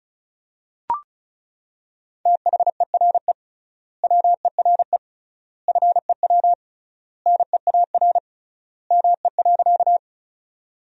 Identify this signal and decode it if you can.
Morse code, heard as a keyed tone, about 36 wpm, 700 Hz. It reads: THERE WERE FEW NEAR ME.
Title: 3x Word-Spacing - Sentences from Top 300 Words - 35wpm